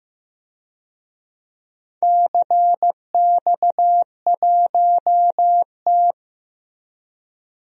Morse code, heard as a keyed tone, about 15 wpm, 700 Hz. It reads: CX1T
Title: DXCC-100 Call Signs - 4 Characters - Repeated 2x - 15wpm